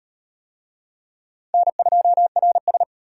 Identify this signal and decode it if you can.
Morse code, heard as a keyed tone, about 34 wpm, 700 Hz. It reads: N2RS